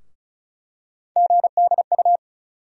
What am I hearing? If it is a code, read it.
Morse code, heard as a keyed tone, about 36 wpm, 700 Hz. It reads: GDU
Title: USA Call Sign Postfixes - 3 Letters 35wpm